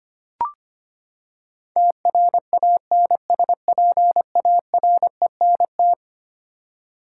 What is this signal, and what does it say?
Morse code, heard as a keyed tone, about 25 wpm, 700 Hz. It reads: TRANSPARENT
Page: morse-code-ninja-practice-25wpm